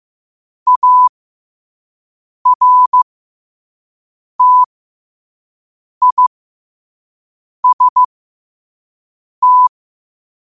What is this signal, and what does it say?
Morse code, heard as a keyed tone, about 15 wpm, spaced as for 5 wpm, 1000 Hz. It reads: ARTIST
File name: qotdfiles / 5wpm